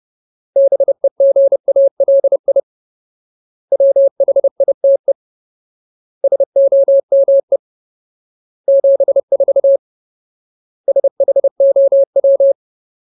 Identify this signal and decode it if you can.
Morse code, heard as a keyed tone, about 30 wpm, 550 Hz. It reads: BEGALI WHITE SOME 74 SHOW
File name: Ultimate Word List / 30WPM Ultimate